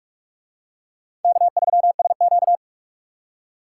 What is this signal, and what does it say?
Morse code, heard as a keyed tone, about 39 wpm, 700 Hz. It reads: K3SQ